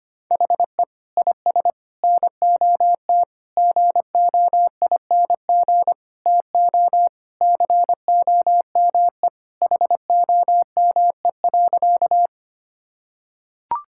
Morse code, heard as a keyed tone, about 25 wpm, 700 Hz. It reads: HE IS NOT GOING TO COME HOME.
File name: practice / 25wpm / Sentences with 4-Word Phrases - 25wpm